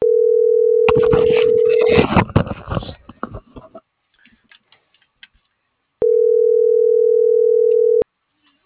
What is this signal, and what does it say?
Ringback tone (North America), 440 + 480 Hz.